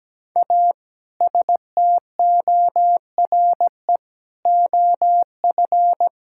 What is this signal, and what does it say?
Morse code, heard as a keyed tone, about 17 wpm, 700 Hz. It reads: A STORE OF
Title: Sentences from Top 1000 Words - Review Part 5 - 17wpm